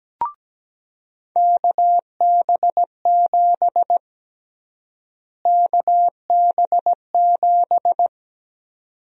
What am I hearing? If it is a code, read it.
Morse code, heard as a keyed tone, about 17 wpm, 700 Hz. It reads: KB7 KB7
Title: USA Call Sign Prefixes - Repeated 2x - 17wpm